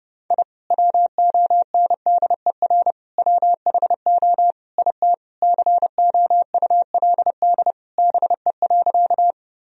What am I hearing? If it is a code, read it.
Morse code, heard as a keyed tone, about 30 wpm, 700 Hz. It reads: I WONDER WHO IT COULD BE.
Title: Sentences from Top 1000 Words - Review Part 3 - 30wpm